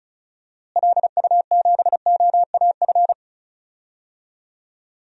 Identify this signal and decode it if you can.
Morse code, heard as a keyed tone, about 34 wpm, 700 Hz. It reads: LU7OAF
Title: DXCC-100 Call Signs - 6 Characters - Repeated 2x - 35wpm